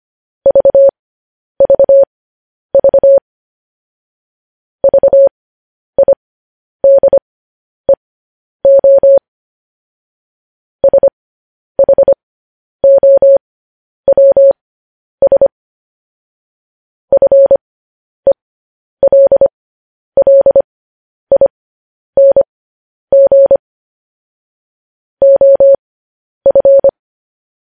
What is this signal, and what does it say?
Morse code, heard as a keyed tone, about 25 wpm, spaced as for 10 wpm, 550 Hz. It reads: VVV VIDEO SHOWS FELLING OF
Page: News Headlines in Morse Code at 10 WPM